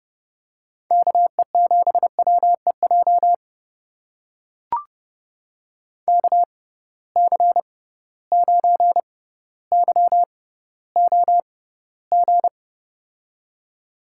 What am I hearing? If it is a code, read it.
Morse code, heard as a keyed tone, about 30 wpm, spaced as for 15 wpm, 700 Hz. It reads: KE7WEJ KC9YOG
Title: USA Call Signs - 6 Characters - 6x Character Spacing - 30wpm